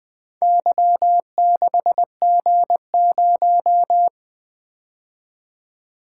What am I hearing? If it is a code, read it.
Morse code, heard as a keyed tone, about 20 wpm, 700 Hz. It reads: Y6G0
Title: Four Letter-Number Combinations - 20wpm